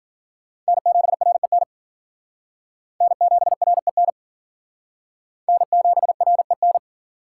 Morse code, heard as a keyed tone, about 49 wpm, 700 Hz. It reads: N7REN N7REN N7REN
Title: Speed Racing - USA 1x3 Call Signs 40wpm